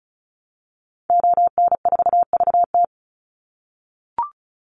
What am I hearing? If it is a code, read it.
Morse code, heard as a keyed tone, about 34 wpm, 700 Hz. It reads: ON4VT
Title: Call Signs - 35wpm